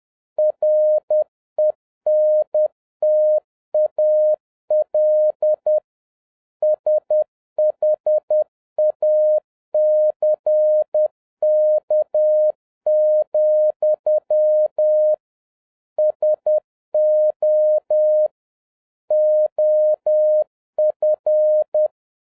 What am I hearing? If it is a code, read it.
Morse code, heard as a keyed tone, about 10 wpm, 610 Hz. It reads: RENTAL SHACK, SO OF